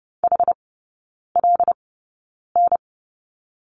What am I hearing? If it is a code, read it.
Morse code, heard as a keyed tone, about 30 wpm, spaced as for 9 wpm, 700 Hz. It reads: HLN
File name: Lesson 16e - Letter c - Three Characters 30wpm